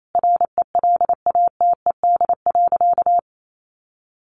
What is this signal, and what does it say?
Morse code, heard as a keyed tone, about 28 wpm, 700 Hz. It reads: RELATED.
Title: Sentences from Top 1400 Words 28wpm